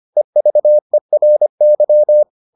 Morse code, heard as a keyed tone, about 25 wpm, 600 Hz. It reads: EVERY